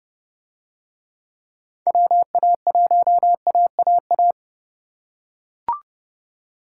Morse code, heard as a keyed tone, about 28 wpm, 700 Hz. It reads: WA1AAA